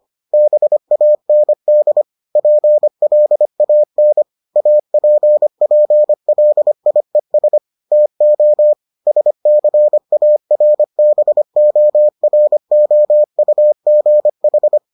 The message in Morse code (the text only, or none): BAND PLAN APPLIES TO SCARBOROUGH